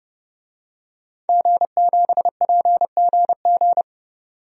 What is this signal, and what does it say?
Morse code, heard as a keyed tone, about 30 wpm, 700 Hz. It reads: G7PGG
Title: DXCC-100 Call Signs - 5 Characters - 30wpm